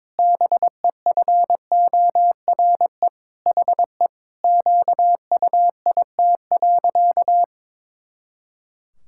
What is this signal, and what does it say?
Morse code, heard as a keyed tone, about 22 wpm, 700 Hz. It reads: BEFORE HE QUIT.
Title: Pangram Challenge v2 - 22wpm